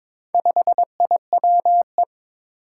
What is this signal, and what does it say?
Morse code, heard as a keyed tone, about 22 wpm, 700 Hz. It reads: 5IWE
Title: Four Challenging Letter-Number Combinations 22wpm